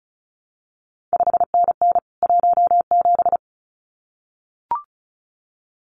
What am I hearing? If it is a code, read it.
Morse code, heard as a keyed tone, about 36 wpm, 700 Hz. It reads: 5NN 17
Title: CQ WW DX Contest Exchange Encore - 35wpm